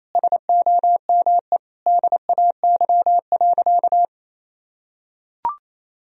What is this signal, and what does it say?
Morse code, heard as a keyed tone, about 27 wpm, 700 Hz. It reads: SOME DAY.
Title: Sentences from Top 1000 Words - Review Part 4 - 28wpm